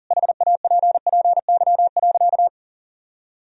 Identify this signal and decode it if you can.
Morse code, heard as a keyed tone, about 40 wpm, 700 Hz. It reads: HAPPY.
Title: Sentences from Top 1000 Words - Review Part 3 - 40wpm